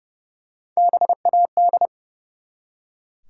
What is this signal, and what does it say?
Morse code, heard as a keyed tone, about 30 wpm, 700 Hz. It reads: BAD
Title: Three Letter Words - 30wpm